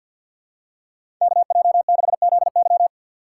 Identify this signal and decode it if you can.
Morse code, heard as a keyed tone, about 48 wpm, 700 Hz. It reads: KJ6ZY
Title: USA Call Signs - 5 Characters - 4x Character Spacing - 50wpm